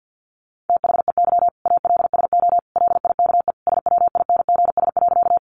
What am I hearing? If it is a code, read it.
Morse code, heard as a keyed tone, about 50 wpm, 700 Hz. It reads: THEY ALSO LIKE SWINGS.
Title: Easy Phrases 50wpm